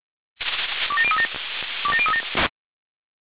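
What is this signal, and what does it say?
Five-tone selective call (ZVEI), heard as quick runs of five tones, tones of about 70 ms. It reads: 27037 27037
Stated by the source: ZVEI five-tone selective call 27002 received on SDR.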